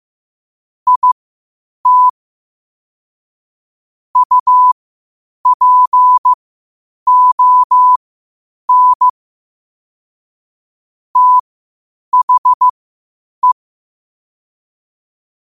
Morse code, heard as a keyed tone, about 15 wpm, spaced as for 8 wpm, 1000 Hz. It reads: IT UPON THE
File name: qotdfiles / 8wpm